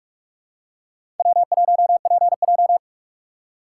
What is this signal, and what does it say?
Morse code, heard as a keyed tone, about 46 wpm, 700 Hz. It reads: W1PJ
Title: USA Call Signs - 4 Characters - 2x Character Spacing - 45wpm